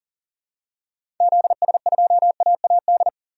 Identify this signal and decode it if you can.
Morse code, heard as a keyed tone, about 40 wpm, 700 Hz. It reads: ZS2AAD
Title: DXCC-100 Call Signs - 6 Characters - 40wpm